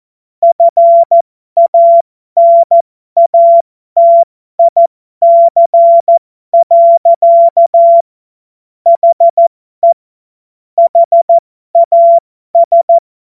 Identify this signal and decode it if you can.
Morse code, heard as a keyed tone, about 14 wpm, spaced as for 12 wpm, 680 Hz. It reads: FANATIC. HE HAS
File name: Morsekurs / Witze